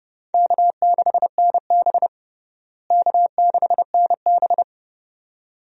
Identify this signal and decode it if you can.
Morse code, heard as a keyed tone, about 32 wpm, 700 Hz. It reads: K6NB K6NB
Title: USA Call Signs - 4 Characters - Repeated 2x - 30wpm